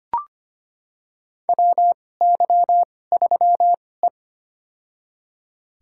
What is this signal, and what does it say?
Morse code, heard as a keyed tone, about 25 wpm, spaced as for 18 wpm, 700 Hz. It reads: WY3E
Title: USA Call Signs - 4 Characters - 2x Character Spacing - 25wpm